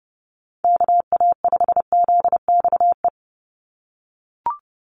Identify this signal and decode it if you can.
Morse code, heard as a keyed tone, about 30 wpm, 700 Hz. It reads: KA5ZXE
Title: USA Call Signs 30wpm